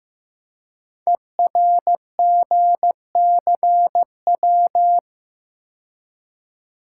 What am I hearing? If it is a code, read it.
Morse code, heard as a keyed tone, about 15 wpm, 700 Hz. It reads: ERGCW
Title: Code Groups - 5 Letter:Numbers - 15wpm